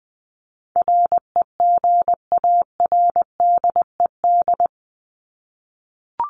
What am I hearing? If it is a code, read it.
Morse code, heard as a keyed tone, about 20 wpm, 700 Hz. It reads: REGARDED
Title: ED Suffix - Top 100 Words with Base Words 20wpm